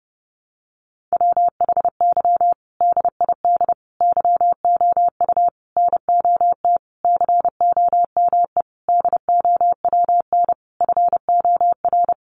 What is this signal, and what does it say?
Morse code, heard as a keyed tone, about 30 wpm, 700 Hz. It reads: WHY DID YOU NOT COME DOWN FOR
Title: Sentences from Top 1500 Words 30wpm